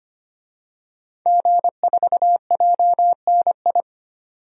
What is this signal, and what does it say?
Morse code, heard as a keyed tone, about 25 wpm, 700 Hz. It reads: G4JNI
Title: DXCC-100 Call Signs - 5 Characters - 25wpm